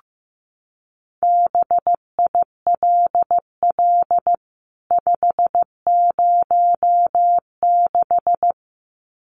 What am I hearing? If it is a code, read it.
Morse code, heard as a keyed tone, about 15 wpm, 700 Hz. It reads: BILL 506
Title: CWT Exchange 15wpm